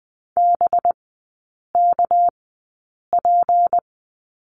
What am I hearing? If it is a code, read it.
Morse code, heard as a keyed tone, about 20 wpm, spaced as for 8 wpm, 690 Hz. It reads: BKP
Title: 198 Lesson 32e - Letter k - Three Characters 20wpm